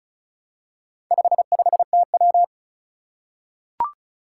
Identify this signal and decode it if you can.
Morse code, heard as a keyed tone, about 36 wpm, 700 Hz. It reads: 55TW
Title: Four Challenging Letter-Number Combinations 35wpm